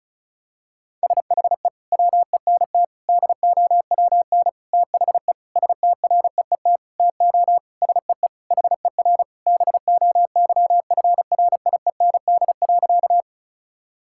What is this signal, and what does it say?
Morse code, heard as a keyed tone, about 36 wpm, 700 Hz. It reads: SHE WENT DOWN THE STREET TO SEE HER BOYFRIEND.
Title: Sentences from Top 1000 Words - Review Part 5 - 35wpm